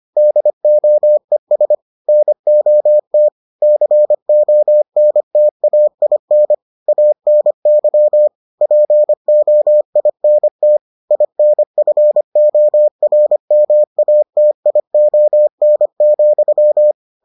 Morse code, heard as a keyed tone, about 25 wpm, 600 Hz. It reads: DOES NOT CONTAIN ANY POINT INFORMATION,